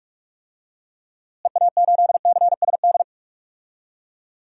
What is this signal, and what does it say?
Morse code, heard as a keyed tone, about 42 wpm, 700 Hz. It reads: EA8CSD
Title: DXCC-100 Call Signs - 6 Characters - Repeated 3x - 45wpm